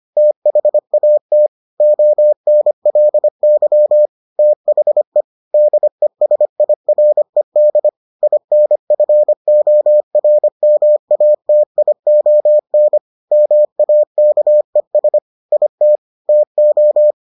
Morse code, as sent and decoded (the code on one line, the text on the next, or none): - .... .- - / --- -. .-.. -.-- / - .... / -.. ... .. .-. -.. / .. -. ..-. --- .-. -- .- - .. --- -. / -- .- -.- . ... / .. - / - ---
THAT ONLY THE DESIRED INFORMATION MAKES IT TO